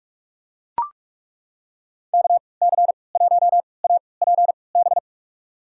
Morse code, heard as a keyed tone, about 42 wpm, spaced as for 25 wpm, 700 Hz. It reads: KC1APB
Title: USA Call Signs - 3x Character Spacing - 45wpm